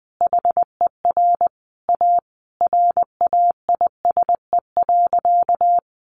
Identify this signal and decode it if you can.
Morse code, heard as a keyed tone, about 20 wpm, 700 Hz. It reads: HER A RAISE.
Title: Easy Phrases 20wpm